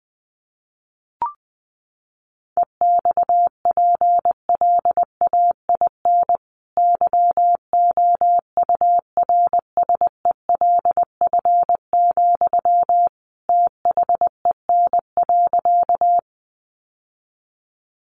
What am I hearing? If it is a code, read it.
Morse code, heard as a keyed tone, about 19 wpm, 700 Hz. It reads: EXPLAIN YOURSELF, THEN.
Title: Sentences from Top 1100 Words 20wpm